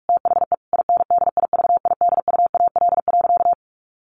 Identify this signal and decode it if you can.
Morse code, heard as a keyed tone, about 44 wpm, 700 Hz. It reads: THE INDIVIDUAL.
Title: Sentences from Top 1400 Words 45wpm